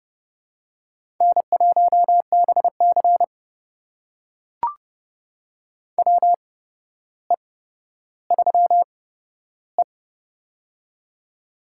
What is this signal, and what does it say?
Morse code, heard as a keyed tone, about 30 wpm, spaced as for 13 wpm, 700 Hz. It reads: N1BC WE3E
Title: USA Call Signs - 4 Characters - 8x Character Spacing - 30wpm